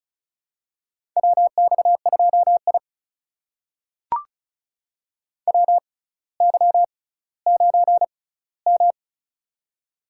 Morse code, heard as a keyed tone, about 36 wpm, spaced as for 18 wpm, 700 Hz. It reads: WX2I WY9M
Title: USA Call Signs - 4 Characters - 6x Character Spacing - 35wpm